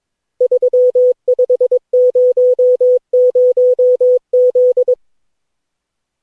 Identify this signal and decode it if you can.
Morse code, heard as a keyed tone, about 22 wpm, 500 Hz. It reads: 3500Z
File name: cw / Session9 cw training